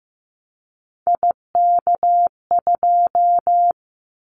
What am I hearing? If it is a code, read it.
Morse code, heard as a keyed tone, about 15 wpm, 700 Hz. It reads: IK2
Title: Callsign Prefixes for Top 25 DXCC - 15wpm